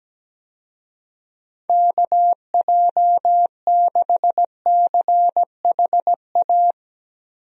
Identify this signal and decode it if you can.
Morse code, heard as a keyed tone, about 17 wpm, 700 Hz. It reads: KJ6CHA
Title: USA Call Signs - 6 Characters - Repeated 2x - 17wpm